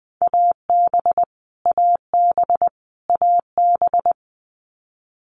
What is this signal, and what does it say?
Morse code, heard as a keyed tone, about 20 wpm, 700 Hz. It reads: AB AB AB
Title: Canadian Provinces and Territories - Repeated 3x - 20wpm